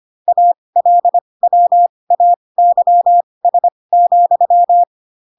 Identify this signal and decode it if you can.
Morse code, heard as a keyed tone, about 25 wpm, spaced as for 20 wpm, 700 Hz. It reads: ALWAYS,
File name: Pinocchio / Pinocchio ch8 15 25 ik1hge